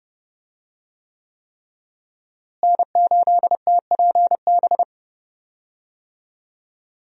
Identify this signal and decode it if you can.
Morse code, heard as a keyed tone, about 30 wpm, 700 Hz. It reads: N8TPB
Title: USA Call Signs - 5 Characters - Repeated 3x - 30wpm